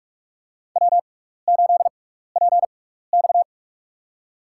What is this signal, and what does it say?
Morse code, heard as a keyed tone, about 46 wpm, spaced as for 16 wpm, 700 Hz. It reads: W8PX
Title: USA Call Signs - 4 Characters - 6x Character Spacing - 45wpm